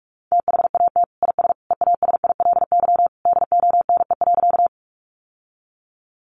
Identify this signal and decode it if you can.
Morse code, heard as a keyed tone, about 44 wpm, 700 Hz. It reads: THAT IS EASILY DONE.
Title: Sentences from Top 900 Words 45wpm